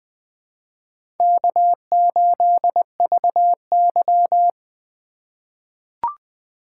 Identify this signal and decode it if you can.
Morse code, heard as a keyed tone, about 21 wpm, 700 Hz. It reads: K8VY
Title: USA Call Signs - 3x Character Spacing - 20wpm